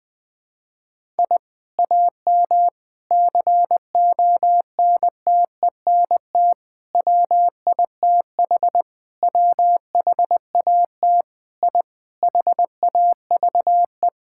Morse code, heard as a keyed tone, about 21 wpm, 700 Hz. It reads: I AM CONTENT WITH WHAT I HAVE
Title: Sentences from Top 1000 Words - Review Part 6 - 20wpm